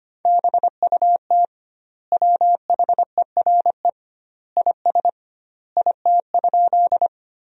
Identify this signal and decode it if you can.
Morse code, heard as a keyed tone, about 25 wpm, 700 Hz. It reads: BUT WHERE IS IT?
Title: 2x Word-Spacing - Sentences from Top 200 Words - 25wpm